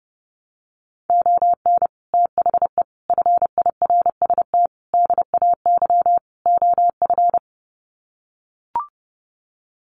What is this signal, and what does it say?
Morse code, heard as a keyed tone, about 28 wpm, 700 Hz. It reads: ON THE FIRST DAY OF
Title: Sets of 5 Words Encore 30wpm